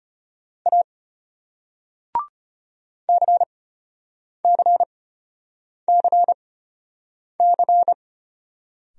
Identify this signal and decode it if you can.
Morse code, heard as a keyed tone, about 33 wpm, spaced as for 8 wpm, 700 Hz. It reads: A CCCC